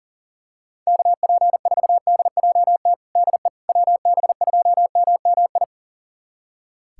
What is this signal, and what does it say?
Morse code, heard as a keyed tone, about 37 wpm, 700 Hz. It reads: KP4DJT DE WB2MMI
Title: USA Callsign DE Callsign - 6 Characters - 40wpm